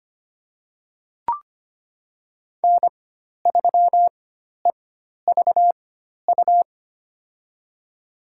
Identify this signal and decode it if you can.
Morse code, heard as a keyed tone, about 25 wpm, spaced as for 12 wpm, 700 Hz. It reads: N3EVU
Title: USA Call Signs - 4x Character Spacing - 25wpm